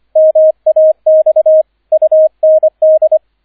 Morse code, heard as a keyed tone, about 25 wpm, 630 Hz, with radio noise. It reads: MAX UND